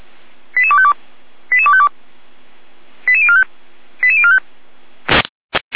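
Five-tone selective call (ZVEI), heard as quick runs of five tones, tones of about 70 ms. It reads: 88252 88252 88045 88045
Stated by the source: Received on an SDR, ZVEI five-tone selective call.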